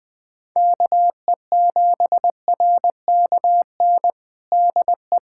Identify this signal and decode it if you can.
Morse code, heard as a keyed tone, about 20 wpm, 700 Hz. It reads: KE7RKN DE
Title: USA Callsign DE Callsign 20wpm